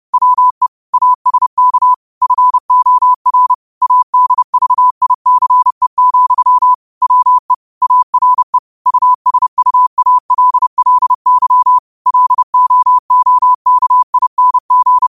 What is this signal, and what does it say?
Morse code, heard as a keyed tone, about 30 wpm, 1000 Hz. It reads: WE ASK FOR ADVICE, WE ARE USUALLY LOOKING
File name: qotdfiles / 30wpm